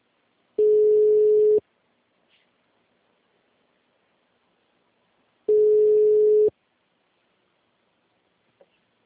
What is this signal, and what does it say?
Ringback tone (Europe, 425 Hz).
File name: out- -2040- - - .10496